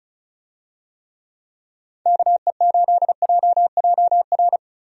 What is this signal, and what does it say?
Morse code, heard as a keyed tone, about 36 wpm, 700 Hz. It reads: KE8JJR
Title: USA Call Signs - 6 Characters - Repeated 3x - 35wpm